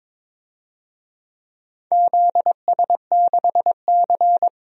Morse code, heard as a keyed tone, about 21 wpm, 700 Hz. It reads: ZS6C
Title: DXCC-100 Call Signs - 4 Characters - Repeated 3x - 22wpm